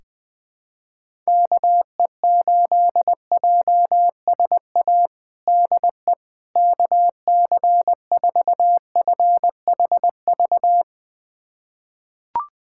Morse code, heard as a keyed tone, about 20 wpm, 700 Hz. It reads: KE8JSA DE KC4FHV
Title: USA Callsign DE Callsign - 6 Characters - 20wpm